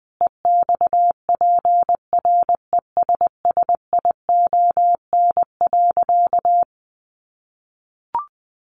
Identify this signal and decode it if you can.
Morse code, heard as a keyed tone, about 20 wpm, 700 Hz. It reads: EXPRESSION.
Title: Sentences from Top 1200 Words 20wpm